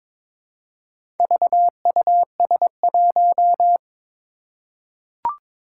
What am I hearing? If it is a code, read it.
Morse code, heard as a keyed tone, about 22 wpm, 700 Hz. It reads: VUS1
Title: Four Challenging Letter-Number Combinations 22wpm